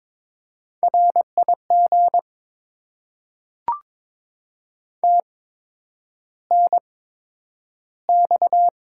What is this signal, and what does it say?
Morse code, heard as a keyed tone, about 22 wpm, spaced as for 9 wpm, 700 Hz. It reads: RIG TNX
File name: Top 20 QSO Elements - 8x Character Spacing - 22wpm